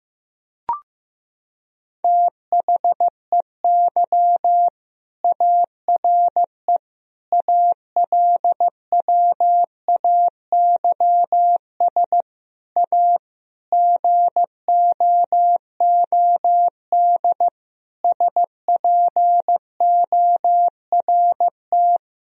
Morse code, heard as a keyed tone, about 15 wpm, 700 Hz. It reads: THEY ARE ALWAYS A GOOD SPORT
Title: Sentences from Top 1000 Words - Review Part 4 - 15wpm